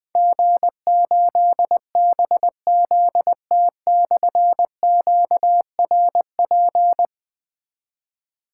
Morse code, heard as a keyed tone, about 21 wpm, 700 Hz. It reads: G8BZT/QRP
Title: Difficult Call Signs 20wpm